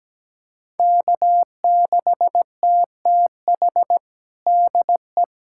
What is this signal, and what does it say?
Morse code, heard as a keyed tone, about 17 wpm, 700 Hz. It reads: K6TTH DE
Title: USA Callsign DE Callsign - 5 Characters - 17wpm